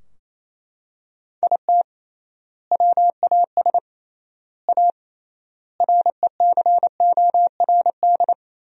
Morse code, heard as a keyed tone, about 28 wpm, 700 Hz. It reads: IT WAS A RECORD
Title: 3x Word-Spacing - Sentences from Top 500 Words - 28wpm